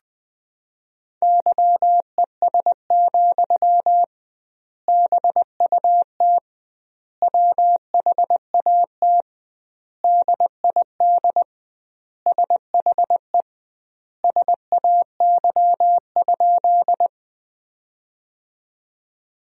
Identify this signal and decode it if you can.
Morse code, heard as a keyed tone, about 20 wpm, 700 Hz. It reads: YES, BUT WHAT DID SHE SAY?
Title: 2x Word-Spacing - Sentences from Top 500 Words - 20wpm